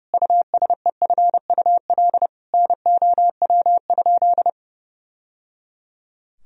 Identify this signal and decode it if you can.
Morse code, heard as a keyed tone, about 30 wpm, 700 Hz. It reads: USEFUL NOW?
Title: Sentences from Top 1000 Words - Review Part 4 - 30wpm